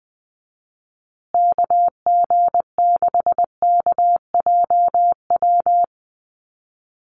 Morse code, heard as a keyed tone, about 20 wpm, 700 Hz. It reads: KG6KJW
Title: USA Call Signs Repeated 2x 20wpm